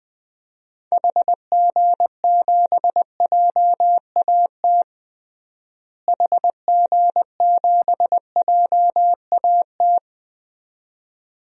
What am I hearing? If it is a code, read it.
Morse code, heard as a keyed tone, about 20 wpm, 700 Hz. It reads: HG7JAT HG7JAT
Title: DXCC-100 Call Signs - 6 Characters - Repeated 2x - 20wpm